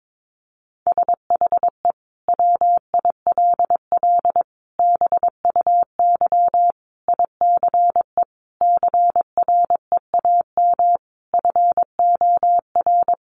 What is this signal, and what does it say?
Morse code, heard as a keyed tone, about 22 wpm, 700 Hz. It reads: SHE WILL BUY ICE CREAM FOR
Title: Sentences from Top 1500 Words 22wpm